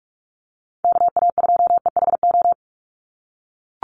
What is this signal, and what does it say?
Morse code, heard as a keyed tone, about 44 wpm, 700 Hz. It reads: KA2EHO
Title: USA Call Signs 45wpm